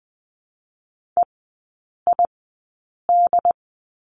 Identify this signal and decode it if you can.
Morse code, heard as a keyed tone, about 20 wpm, spaced as for 8 wpm, 700 Hz. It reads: EID
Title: 121 Lesson 20e - Number 3 - Three Characters 20wpm